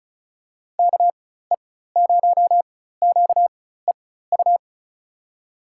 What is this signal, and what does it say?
Morse code, heard as a keyed tone, about 34 wpm, spaced as for 16 wpm, 700 Hz. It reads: KE0QEU
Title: USA Call Signs - 6 Characters - 4x Character Spacing - 35wpm